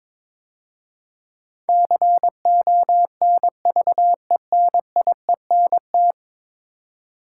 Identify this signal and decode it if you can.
Morse code, heard as a keyed tone, about 22 wpm, 700 Hz. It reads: CONVENIENT
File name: practice / 22wpm / 10 Letter Words - First 3 Letters Given - 22wpm